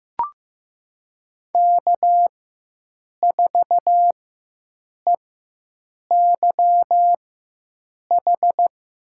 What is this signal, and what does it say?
Morse code, heard as a keyed tone, about 15 wpm, spaced as for 7 wpm, 700 Hz. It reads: K4EYH
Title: USA Call Signs - 4x Character Spacing - 15wpm